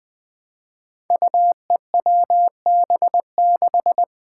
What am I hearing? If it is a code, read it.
Morse code, heard as a keyed tone, about 21 wpm, 700 Hz. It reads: UEWB6
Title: Five Challenging Letter-Number Combinations 20wpm